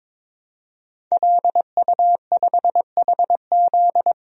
Morse code, heard as a keyed tone, about 22 wpm, 700 Hz. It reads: LU5HZ
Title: DXCC-100 Call Signs - 5 Characters - Repeated 2x - 22wpm